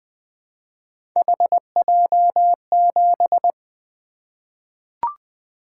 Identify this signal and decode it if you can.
Morse code, heard as a keyed tone, about 20 wpm, 700 Hz. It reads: HJ7
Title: Three Letter-Number Combinations - 20wpm